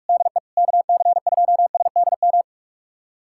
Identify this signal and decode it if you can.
Morse code, heard as a keyed tone, about 46 wpm, 700 Hz. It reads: DE KC2SDM
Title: USA Callsign DE Callsign - 6 Characters - 45wpm